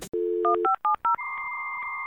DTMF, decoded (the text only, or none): 1570